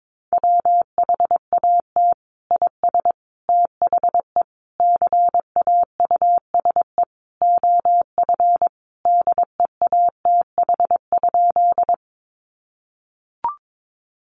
Morse code, heard as a keyed tone, about 22 wpm, 700 Hz. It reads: WHAT IS THE CAUSE OF DEATH?
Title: Sentences from Top 1100 Words 22wpm